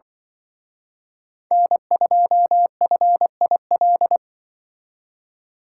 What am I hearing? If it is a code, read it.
Morse code, heard as a keyed tone, about 24 wpm, 700 Hz. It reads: N2FIL
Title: Speed Racing - USA 1x3 Call Signs 20wpm